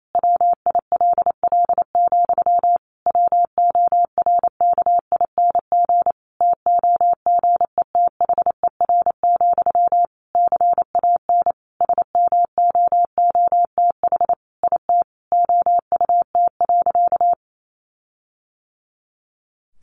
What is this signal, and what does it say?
Morse code, heard as a keyed tone, about 28 wpm, 700 Hz. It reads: WILL, WORKING TOGETHER, CAN SMOOTH IT OUT.
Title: Sentences from Top 1400 Words 28wpm